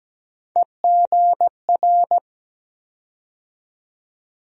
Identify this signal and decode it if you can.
Morse code, heard as a keyed tone, about 17 wpm, 700 Hz. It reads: EGR
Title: USA Call Sign Postfixes - 3 Letters 17wpm